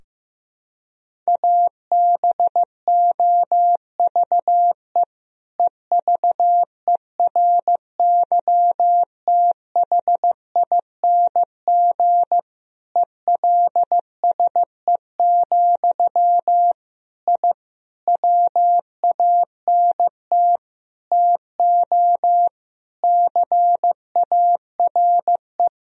Morse code, heard as a keyed tone, about 15 wpm, 700 Hz. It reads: ABOVE EVERYTHING ELSE, I WANT TO CARE
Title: Sentences from Top 1000 Words - Review Part 2 - 15wpm